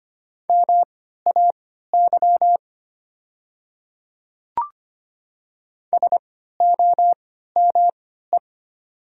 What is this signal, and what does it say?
Morse code, heard as a keyed tone, about 25 wpm, spaced as for 14 wpm, 700 Hz. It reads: MAY SOME
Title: 3x Character Spacing - Top 100 Words - 25wpm